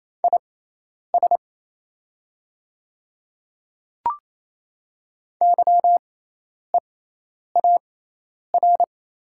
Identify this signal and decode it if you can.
Morse code, heard as a keyed tone, about 28 wpm, spaced as for 10 wpm, 700 Hz. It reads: IS YEAR
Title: 6x Character Spacing - Top 300 Words - 28wpm